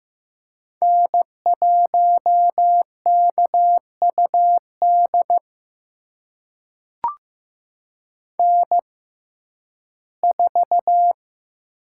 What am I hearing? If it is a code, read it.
Morse code, heard as a keyed tone, about 15 wpm, spaced as for 11 wpm, 700 Hz. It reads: N1KUD N4
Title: USA Call Signs - 5 Characters - 6x Character Spacing - 15wpm